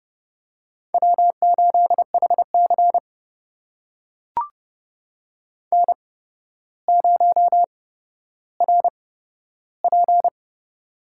Morse code, heard as a keyed tone, about 32 wpm, spaced as for 13 wpm, 700 Hz. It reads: W8HC N0RP
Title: USA Call Signs - 4 Characters - 8x Character Spacing - 30wpm